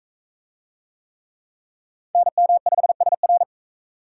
Morse code, heard as a keyed tone, about 41 wpm, 700 Hz. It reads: NM5SR